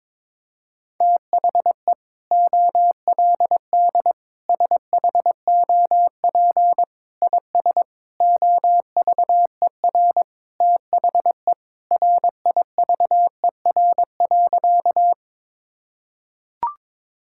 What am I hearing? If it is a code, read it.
Morse code, heard as a keyed tone, about 21 wpm, 700 Hz. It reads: THE OLD SHOP IS OVER THE RIVER.
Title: Sentences from Top 1600 Words 22wpm